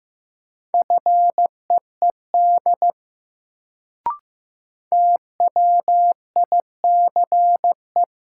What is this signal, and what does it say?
Morse code, heard as a keyed tone, about 15 wpm, 700 Hz. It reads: FEED TWICE
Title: Sentences from Top 1600 Words - New Word Drill Alt - 15wpm